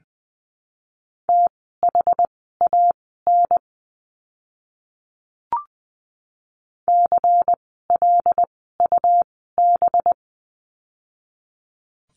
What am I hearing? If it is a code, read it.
Morse code, heard as a keyed tone, about 20 wpm, spaced as for 14 wpm, 700 Hz. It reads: THAN CLUB
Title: 185 Lesson 30f - Letter b - Words 20wpm